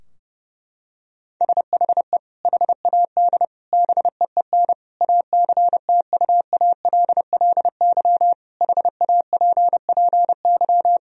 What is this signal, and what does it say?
Morse code, heard as a keyed tone, about 30 wpm, 700 Hz. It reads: SHE HAD BEEN ACTUALLY HAPPY
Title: Sentences from Top 1000 Words - Review Part 4 - 30wpm